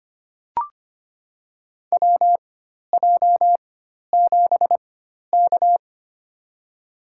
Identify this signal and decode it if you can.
Morse code, heard as a keyed tone, about 25 wpm, spaced as for 12 wpm, 700 Hz. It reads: WJ7K
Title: USA Call Signs - 4 Characters - 4x Character Spacing - 25wpm